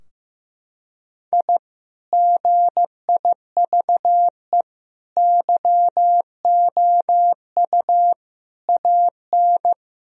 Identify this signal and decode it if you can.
Morse code, heard as a keyed tone, about 15 wpm, 700 Hz. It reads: I GIVE YOU AN
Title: Sentences from Top 1000 Words - Review Part 2 - 15wpm